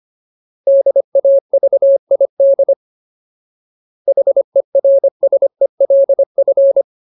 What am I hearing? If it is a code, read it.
Morse code, heard as a keyed tone, about 25 wpm, 550 Hz. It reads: DAVID HERSELF
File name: Ultimate Word List / 25WPM Ultimate